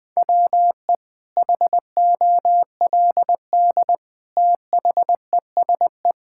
Morse code, heard as a keyed tone, about 20 wpm, 700 Hz. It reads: WE HOLD THESE
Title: Sentences with 9-Word Phrases - 20wpm